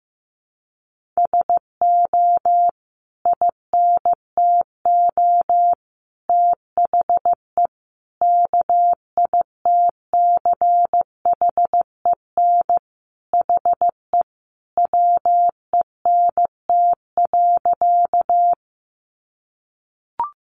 Morse code, heard as a keyed tone, about 15 wpm, 700 Hz. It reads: SO INTO THE KITCHEN HE WENT.
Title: Sentences from Top 1400 Words 15wpm